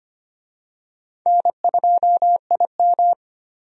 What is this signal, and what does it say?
Morse code, heard as a keyed tone, about 25 wpm, 700 Hz. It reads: N2IM
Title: USA Call Signs - 4 Characters - 25wpm